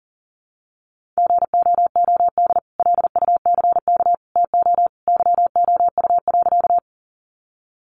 Morse code, heard as a keyed tone, about 40 wpm, 700 Hz. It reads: GOOD LUCK TO YOU.
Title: Sentences from Top 1500 Words 40wpm